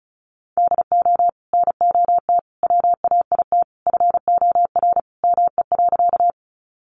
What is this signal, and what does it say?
Morse code, heard as a keyed tone, about 34 wpm, 700 Hz. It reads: DO NOT WAIT FOR ME.
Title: Sentences from Top 600 Words 35wpm